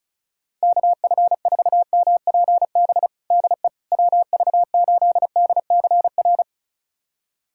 Morse code, heard as a keyed tone, about 36 wpm, 700 Hz. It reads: KF4MPB DE WV8DCR
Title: USA Callsign DE Callsign 35wpm